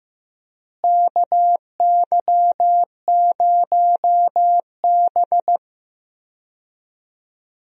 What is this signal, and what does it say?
Morse code, heard as a keyed tone, about 15 wpm, 700 Hz. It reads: KY0B
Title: USA Call Signs - 4 Characters - Repeated 3x - 15wpm